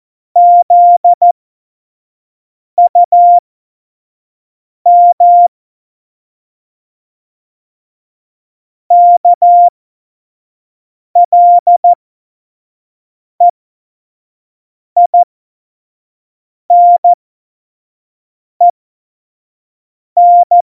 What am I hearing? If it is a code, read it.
Morse code, heard as a keyed tone, about 14 wpm, spaced as for 5 wpm, 700 Hz. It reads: ZUM KLEINEN
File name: Morsekurs / Witze / Witz 27. Februar 2018, 25 F70